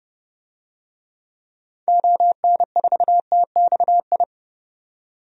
Morse code, heard as a keyed tone, about 31 wpm, 700 Hz. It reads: ON4TXI
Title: DXCC-100 Call Signs - 6 Characters - 30wpm